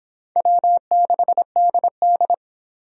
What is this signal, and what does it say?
Morse code, heard as a keyed tone, about 23 wpm, 700 Hz. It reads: W6DD